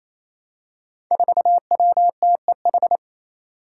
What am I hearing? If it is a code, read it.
Morse code, heard as a keyed tone, about 28 wpm, 700 Hz. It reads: 4WTEH
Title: Five Challenging Letter-Number Combinations 28wpm